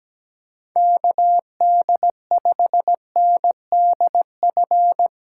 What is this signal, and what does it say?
Morse code, heard as a keyed tone, about 17 wpm, 700 Hz. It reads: KD5NDF
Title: USA Call Signs - 6 Characters - 3x Character Spacing - 17wpm